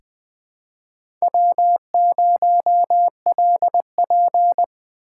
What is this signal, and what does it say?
Morse code, heard as a keyed tone, about 20 wpm, 700 Hz. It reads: W0LP
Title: USA Call Signs - 4 Characters - 6x Character Spacing - 20wpm